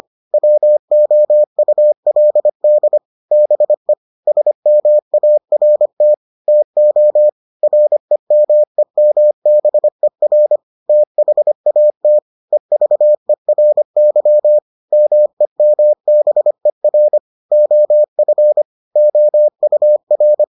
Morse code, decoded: WOULD BE SMART TO REMEMBER THAT EVERY MEMBER OF OUR

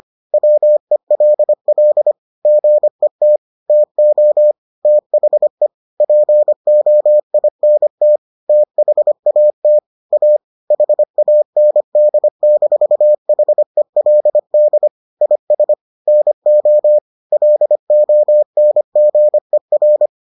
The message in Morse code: WELL GET TO THE POINT THAT A HAND-HELD IS NO LONGER

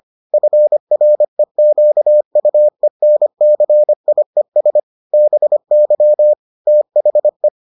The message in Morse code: FREQUENCIES BY THE

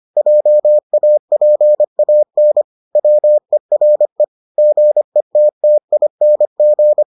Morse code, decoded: JAPAN WERE GETTING

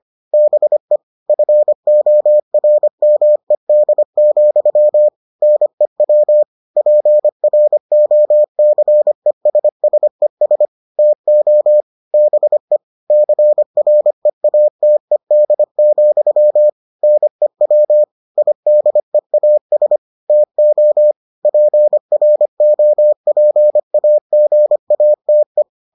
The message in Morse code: BE FORMED, NEW PROCESSES TO BE CREATED, NEW IDEAS TO PROPAGATE

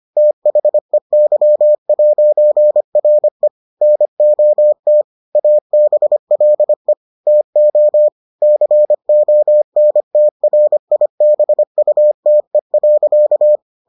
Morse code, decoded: THEY'RE NOT ABLE TO CONTRIBUTE.

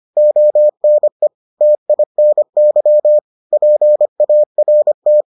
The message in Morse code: ONE TINY PART

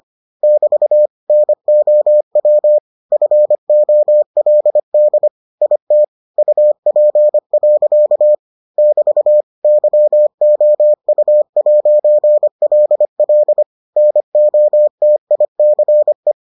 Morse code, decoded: <BT> NOW FOLD IT UP. <BT> YOU'LL NOTICE

